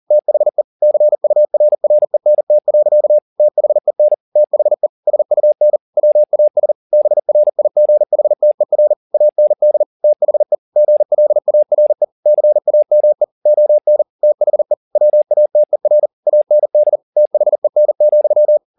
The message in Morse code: THE CURRENT. THEN THE SUN WAS BRIGHTER AND THE GLARE CAME ON THE WATER AND THEN,

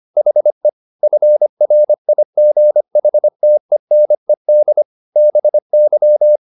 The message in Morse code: HE FRIGHTENED BY

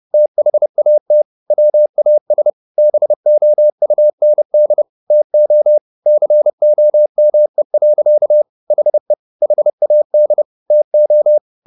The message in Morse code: THAT WAS BOUND TO COME. HE HAD TO

